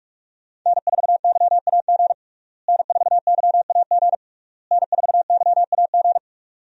Morse code, heard as a keyed tone, about 46 wpm, 700 Hz. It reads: N4YAG N4YAG N4YAG
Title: USA Call Signs - 5 Characters - Repeated 3x - 45wpm